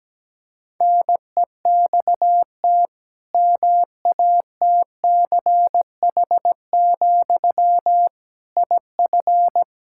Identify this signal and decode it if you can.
Morse code, heard as a keyed tone, about 17 wpm, 700 Hz. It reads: NEXT MATCH, IF
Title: Sentences from Top 1600 Words 17wpm